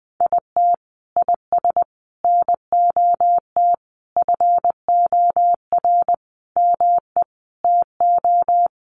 Morse code, heard as a keyed tone, about 20 wpm, 700 Hz. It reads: IT IS NOT FOR ME TO
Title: Sentences from Top 600 Words 20wpm